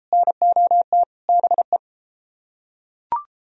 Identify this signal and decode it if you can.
Morse code, heard as a keyed tone, about 33 wpm, 700 Hz. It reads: NOT BE